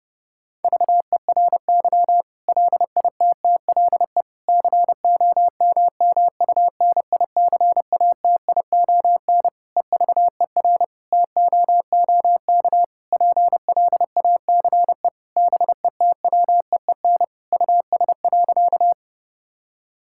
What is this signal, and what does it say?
Morse code, heard as a keyed tone, about 30 wpm, 700 Hz. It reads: VERY LITTLE COMMUNICATION EVER TOOK PLACE BETWEEN US.
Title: Sentences from Top 1000 Words - Review Part 6 - 30wpm